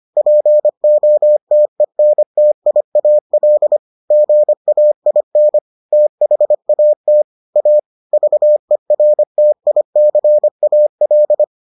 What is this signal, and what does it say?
Morse code, heard as a keyed tone, about 25 wpm, 600 Hz. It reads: POTENTIAL GAIN THAT A VERTICAL